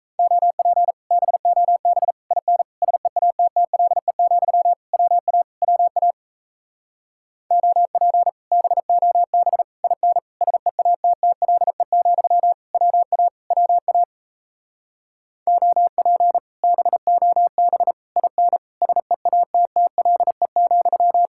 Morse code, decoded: OP BOB IN SEATTLE, WA WA OP BOB IN SEATTLE, WA WA OP BOB IN SEATTLE,

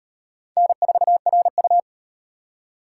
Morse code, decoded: N4RU

